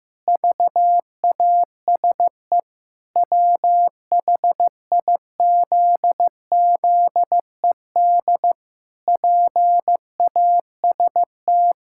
Morse code, heard as a keyed tone, about 15 wpm, 700 Hz. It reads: VASE WHIZZED PAST
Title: Pangram Challenge v2 - 15wpm